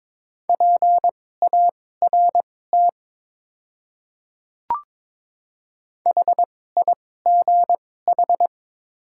Morse code, heard as a keyed tone, about 23 wpm, spaced as for 16 wpm, 700 Hz. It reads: PART HIGH
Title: 2x Character Spacing - Top 500 Words - 22wpm